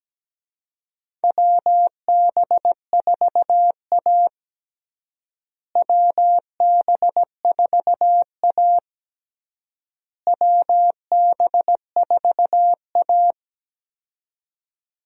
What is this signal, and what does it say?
Morse code, heard as a keyed tone, about 17 wpm, 700 Hz. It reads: WB4A WB4A WB4A
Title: USA Call Signs - 4 Characters - Repeated 3x - 17wpm